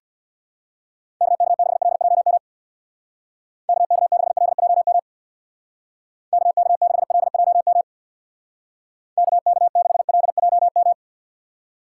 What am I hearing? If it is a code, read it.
Morse code, heard as a keyed tone, about 65 wpm, 700 Hz. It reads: KK6LJM KK6LJM KK6LJM KK6LJM